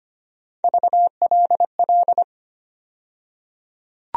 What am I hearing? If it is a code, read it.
Morse code, heard as a keyed tone, about 25 wpm, 700 Hz. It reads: VLL
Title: USA Call Sign Postfixes - 3 Letters 25wpm